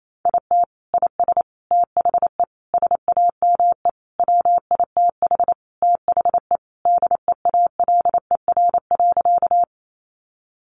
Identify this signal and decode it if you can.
Morse code, heard as a keyed tone, about 28 wpm, 700 Hz. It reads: IT IS THE SAME WITH THE DEALER.
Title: Sentences from Top 1100 Words 28wpm